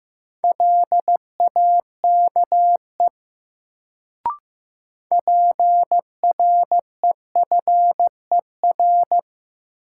Morse code, heard as a keyed tone, about 15 wpm, 700 Hz. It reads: LAKE PREFER
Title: Sentences from Top 1100 Words - New Word Drill Alt - 15wpm